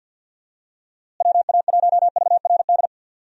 Morse code, heard as a keyed tone, about 45 wpm, 700 Hz. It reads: WA1VRD
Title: USA Call Signs - 6 Characters - Repeated 3x - 50wpm